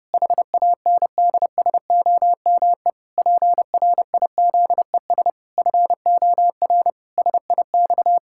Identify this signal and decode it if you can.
Morse code, heard as a keyed tone, about 30 wpm, 700 Hz. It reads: HANDSOME PRIZES FOR SIX
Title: Pangram Challenge v2 - 30wpm